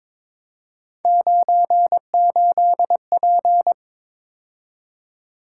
Morse code, heard as a keyed tone, about 26 wpm, 700 Hz. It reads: 98P
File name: Speed-Racing - Three Letter-Number Combinations - 22wpm